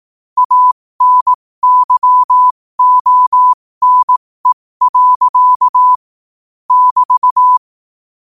Morse code, heard as a keyed tone, about 18 wpm, spaced as for 15 wpm, 1000 Hz. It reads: ANYONE. <BT>